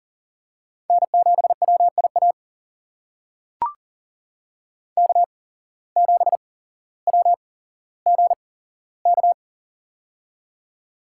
Morse code, heard as a keyed tone, about 37 wpm, spaced as for 17 wpm, 700 Hz. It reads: N7WIA K7WGK
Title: USA Call Signs - 5 Characters - 8x Character Spacing - 40wpm